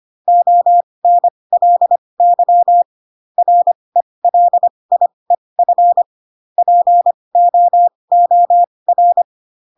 Morse code, heard as a keyed tone, about 25 wpm, spaced as for 20 wpm, 700 Hz. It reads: ONLY RELIEF POOR